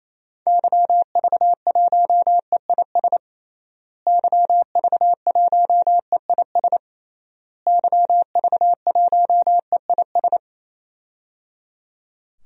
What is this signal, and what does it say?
Morse code, heard as a keyed tone, about 27 wpm, 700 Hz. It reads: YV1EIS YV1EIS YV1EIS
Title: DXCC-100 Call Signs - 6 Characters - Repeated 3x - 28wpm